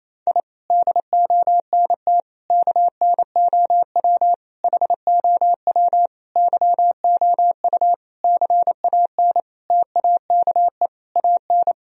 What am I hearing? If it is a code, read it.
Morse code, heard as a keyed tone, about 28 wpm, 700 Hz. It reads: I DONT KNOW HOW YOU CAN TAKE AN
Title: Sentences from Top 1000 Words - Review Part 2 - 28wpm